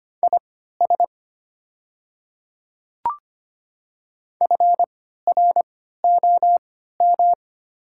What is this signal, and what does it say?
Morse code, heard as a keyed tone, about 25 wpm, spaced as for 14 wpm, 700 Hz. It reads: IS FROM
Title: 3x Character Spacing - Top 100 Words - 25wpm